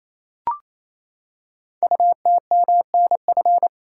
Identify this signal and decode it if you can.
Morse code, heard as a keyed tone, about 28 wpm, 700 Hz. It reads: UTMNF
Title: Code Groups - 5 Letters - 28wpm